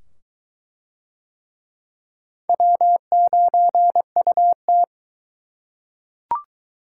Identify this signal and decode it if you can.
Morse code, heard as a keyed tone, about 22 wpm, 700 Hz. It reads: W9UT